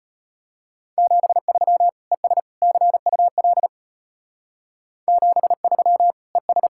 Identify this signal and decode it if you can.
Morse code, heard as a keyed tone, about 32 wpm, 700 Hz. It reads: 73 ES CUL 73 ES